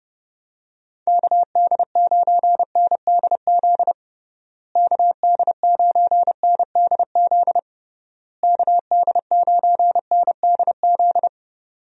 Morse code, heard as a keyed tone, about 30 wpm, 700 Hz. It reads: KD9NDZ KD9NDZ KD9NDZ
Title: USA Call Signs - 6 Characters - Repeated 3x - 30wpm